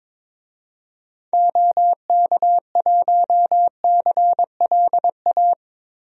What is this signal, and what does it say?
Morse code, heard as a keyed tone, about 22 wpm, 700 Hz. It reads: OK1CLA
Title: DXCC-100 Call Signs - 6 Characters - Repeated 3x - 22wpm